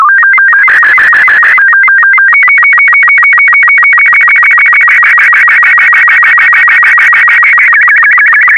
SSTV (slow-scan television), Robot 36.
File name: sstv / R36-VK4MSL